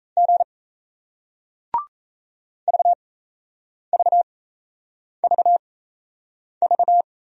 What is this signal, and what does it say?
Morse code, heard as a keyed tone, about 38 wpm, spaced as for 8 wpm, 700 Hz. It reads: G VVVV